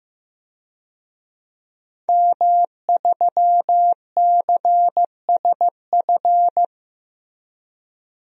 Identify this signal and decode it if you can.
Morse code, heard as a keyed tone, about 15 wpm, 700 Hz. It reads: M3CSF
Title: DXCC-100 Call Signs - 5 Characters - 15wpm